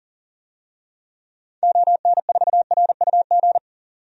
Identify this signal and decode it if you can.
Morse code, heard as a keyed tone, about 40 wpm, 700 Hz. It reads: ON4RUG
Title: DXCC-100 Call Signs - 6 Characters - 40wpm